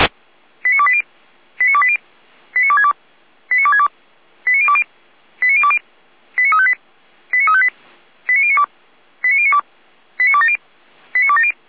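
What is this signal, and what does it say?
Five-tone selective call (ZVEI), heard as quick runs of five tones, tones of about 70 ms. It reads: 89280 89280 89262 89262 89020 89020 89358 89358 89092 89092 89270 89270
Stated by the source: ZVEI five-tone call received by SDR, archived by feuerwehr-orlamuende.org (Orlamünde fire brigade).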